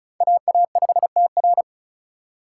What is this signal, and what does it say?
Morse code, heard as a keyed tone, about 36 wpm, 700 Hz. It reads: AA5TR
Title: USA Call Signs - 5 Characters - 2x Character Spacing - 35wpm